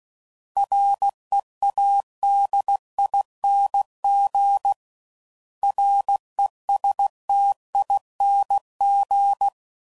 Morse code, heard as a keyed tone, about 16 wpm, 800 Hz. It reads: READING RESTING
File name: Suffixes / Special INGs / Sngl Ings